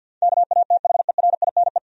Morse code, heard as a keyed tone, about 46 wpm, 700 Hz. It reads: KATHERINE
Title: USA Common Names - 50wpm